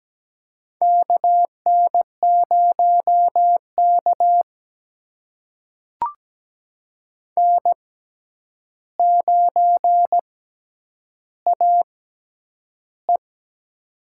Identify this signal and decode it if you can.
Morse code, heard as a keyed tone, about 17 wpm, spaced as for 9 wpm, 700 Hz. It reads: KN0K N9AE
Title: USA Call Signs - 4 Characters - 6x Character Spacing - 17wpm